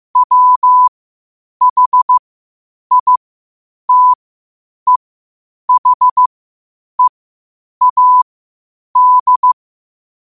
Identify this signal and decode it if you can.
Morse code, heard as a keyed tone, about 15 wpm, spaced as for 8 wpm, 1000 Hz. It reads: WHITEHEAD
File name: qotdfiles / 8wpm